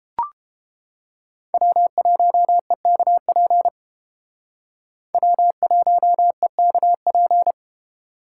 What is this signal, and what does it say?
Morse code, heard as a keyed tone, about 32 wpm, 700 Hz. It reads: W1EKP W1EKP